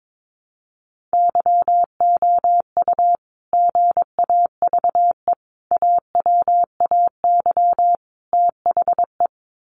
Morse code, heard as a keyed tone, about 22 wpm, 700 Hz. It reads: YOU GAVE AWAY THE
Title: Sentences from Top 1300 Words 22wpm